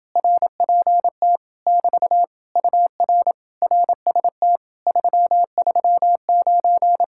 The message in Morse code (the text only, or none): RPT <BT> UR RST 339